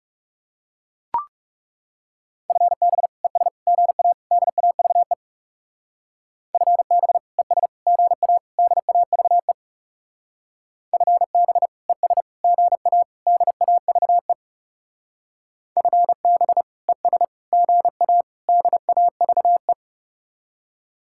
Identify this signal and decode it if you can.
Morse code, heard as a keyed tone, about 36 wpm, 700 Hz. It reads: FB ES GA DAVE FB ES GA DAVE FB ES GA DAVE FB ES GA DAVE
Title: Speed-Racing Common QSO Phrases Encore - 30wpm